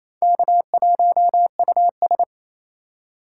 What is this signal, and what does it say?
Morse code, heard as a keyed tone, about 38 wpm, 700 Hz. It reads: K1US